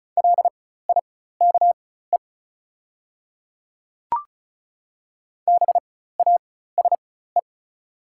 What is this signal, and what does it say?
Morse code, heard as a keyed tone, about 36 wpm, spaced as for 16 wpm, 690 Hz. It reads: LIKE BASE
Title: 4x Character Spacing - Top 300 Words - 35wpm